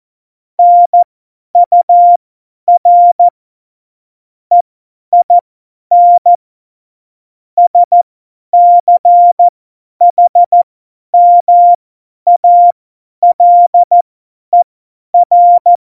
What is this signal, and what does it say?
Morse code, heard as a keyed tone, about 14 wpm, spaced as for 10 wpm, 700 Hz. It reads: NUR EIN SCHMALER
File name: Morsekurs / Witze / Witz 10.Juli 2018, 50 F70